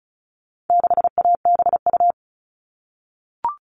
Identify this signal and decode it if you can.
Morse code, heard as a keyed tone, about 34 wpm, 700 Hz. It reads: 6ABU
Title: All Cut Numbers - 4-Digit 35wpm